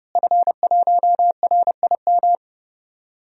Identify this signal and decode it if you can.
Morse code, heard as a keyed tone, about 30 wpm, 700 Hz. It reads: F1RIM
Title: DXCC-100 Call Signs - 5 Characters - Repeated 2x - 30wpm